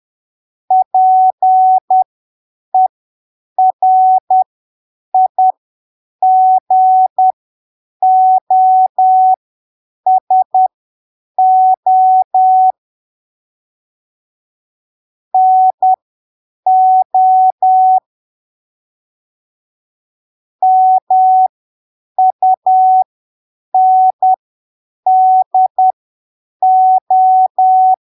Morse code, heard as a keyed tone, about 10 wpm, spaced as for 7 wpm, 750 Hz. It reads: PERIGOSO NO MUNDO
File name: arquivos / texto / texto 24 10ppm